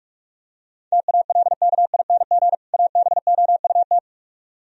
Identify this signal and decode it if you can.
Morse code, heard as a keyed tone, about 45 wpm, 700 Hz. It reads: TALKING ABOUT